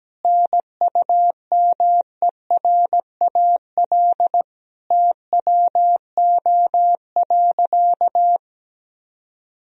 Morse code, heard as a keyed tone, about 17 wpm, 700 Hz. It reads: NUMERAL TWO.
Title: Sentences from Top 1000 Words - Review Part 2 - 17wpm